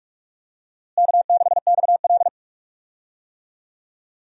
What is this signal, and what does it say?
Morse code, heard as a keyed tone, about 45 wpm, 700 Hz. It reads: K6XL
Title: USA Call Signs - 4 Characters - Repeated 2x - 45wpm